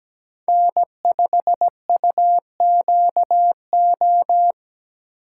Morse code, heard as a keyed tone, about 17 wpm, 700 Hz. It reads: N5UQO
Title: USA Call Signs - 5 Characters - 4x Character Spacing - 17wpm